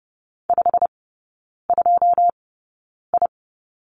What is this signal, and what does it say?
Morse code, heard as a keyed tone, about 30 wpm, spaced as for 9 wpm, 700 Hz. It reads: 52I
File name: course / 30:10wpm / 087 Lesson 15e - Number 5 - Three Characters 30wpm